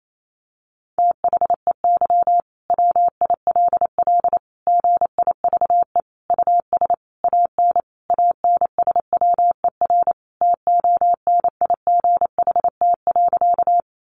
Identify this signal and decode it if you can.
Morse code, heard as a keyed tone, about 28 wpm, 700 Hz. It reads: THEY WILL GIVE US AN ANSWER TONIGHT.
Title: Sentences from Top 600 Words 28wpm